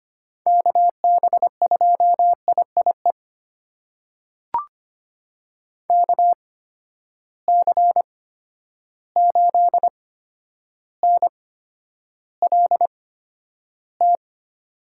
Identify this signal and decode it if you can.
Morse code, heard as a keyed tone, about 25 wpm, spaced as for 11 wpm, 700 Hz. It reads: KB2IIE KC8NLT
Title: USA Call Signs - 6 Characters - 8x Character Spacing - 25wpm